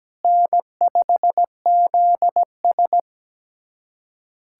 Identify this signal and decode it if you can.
Morse code, heard as a keyed tone, about 17 wpm, 700 Hz. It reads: N5ZS
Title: USA Callsign DE Callsign - 4 Characters - 17wpm